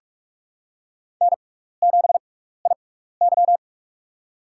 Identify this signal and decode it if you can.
Morse code, heard as a keyed tone, about 44 wpm, spaced as for 15 wpm, 700 Hz. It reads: N7IY
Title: USA Call Signs - 4 Characters - 6x Character Spacing - 45wpm